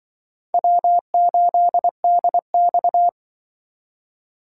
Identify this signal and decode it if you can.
Morse code, heard as a keyed tone, about 24 wpm, 700 Hz. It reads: W8DX
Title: Speed Racing - USA 1x2 Call Signs 20wpm